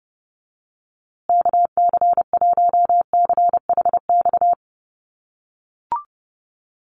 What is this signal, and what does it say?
Morse code, heard as a keyed tone, about 30 wpm, 700 Hz. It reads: KC1CHX
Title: USA Call Signs Repeated 3x 30wpm